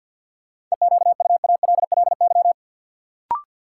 Morse code, heard as a keyed tone, about 53 wpm, 700 Hz. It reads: EQUALLY
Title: Sentences from Top 1400 Words - New Word Drill - 50wpm